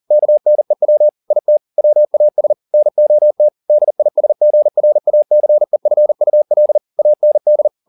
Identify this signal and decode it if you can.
Morse code, heard as a keyed tone, about 40 wpm, 600 Hz. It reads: KNEW IT WAS NOT DISGRACEFUL AND